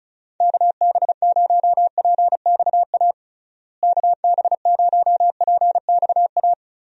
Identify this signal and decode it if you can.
Morse code, heard as a keyed tone, about 34 wpm, 700 Hz. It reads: KB0PXA KB0PXA
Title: USA Call Signs - 6 Characters - Repeated 3x - 35wpm